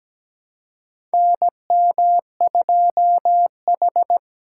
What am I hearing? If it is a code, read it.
Morse code, heard as a keyed tone, about 17 wpm, 700 Hz. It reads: NM2H
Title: USA Call Signs - 3x Character Spacing - 17wpm